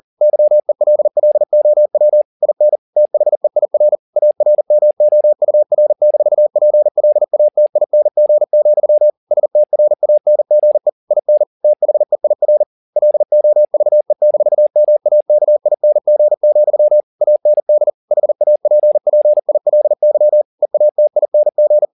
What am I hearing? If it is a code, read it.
Morse code, heard as a keyed tone, about 40 wpm, 600 Hz. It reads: YELLOW IN THEIR ARMOUR-PLATING, STRANGE IN THEIR LOVE-MAKING, AND HAPPILY EATING